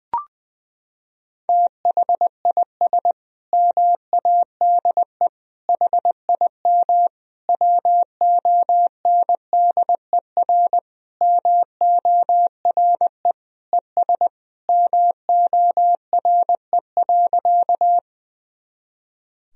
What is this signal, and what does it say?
Morse code, heard as a keyed tone, about 20 wpm, 700 Hz. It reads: THIS MADE HIM WONDER MORE ES MORE.
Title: Sentences from Top 500 Words and Abbreviations 20wpm